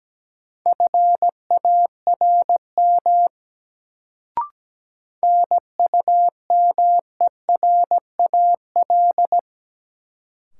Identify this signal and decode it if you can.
Morse code, heard as a keyed tone, about 17 wpm, 700 Hz. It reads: FARM NUMERAL
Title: Sentences from Top 1000 Words - Review Part 2 - New Word Drill - 17wpm